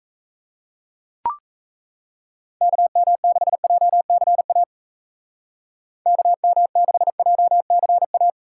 Morse code, heard as a keyed tone, about 34 wpm, 700 Hz. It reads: KM6JCA KM6JCA
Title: Speed Racing - USA Call Signs 28wpm